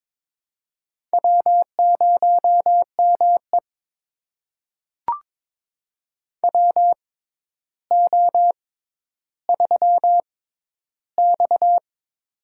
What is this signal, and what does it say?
Morse code, heard as a keyed tone, about 22 wpm, spaced as for 11 wpm, 700 Hz. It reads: W0ME WO3X
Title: USA Call Signs - 4 Characters - 6x Character Spacing - 22wpm